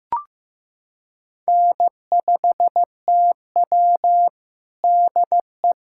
Morse code, heard as a keyed tone, about 15 wpm, 700 Hz. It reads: N5TW DE
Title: USA Callsign DE Callsign - 4 Characters - 15wpm